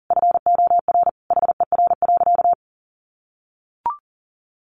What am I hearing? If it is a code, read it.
Morse code, heard as a keyed tone, about 40 wpm, 700 Hz. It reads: FOR HER.
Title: Sentences from Top 1500 Words 40wpm